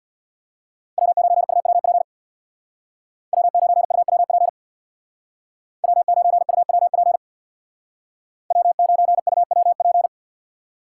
Morse code, heard as a keyed tone, about 61 wpm, 700 Hz. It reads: W9UWP W9UWP W9UWP W9UWP
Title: Speed Racing - USA 1x3 Call Signs 50wpm